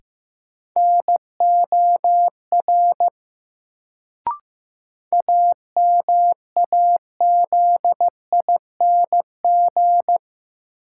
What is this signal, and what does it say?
Morse code, heard as a keyed tone, about 16 wpm, 700 Hz. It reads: NOR AMAZING
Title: Sentences from Top 1500 Words - New Word Drill - 15wpm